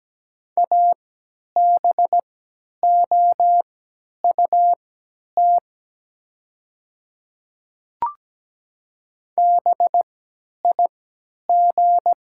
Morse code, heard as a keyed tone, about 17 wpm, spaced as for 10 wpm, 700 Hz. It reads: ABOUT BIG
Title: 3x Character Spacing - Top 200 Words - 17wpm